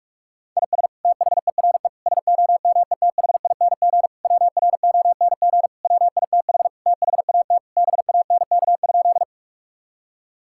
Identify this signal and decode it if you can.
Morse code, heard as a keyed tone, about 46 wpm, 700 Hz. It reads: IS THERE SOMETHING WRONG WITH THAT BANK?
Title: Sentences from Top 1000 Words - Review Part 6 - 45wpm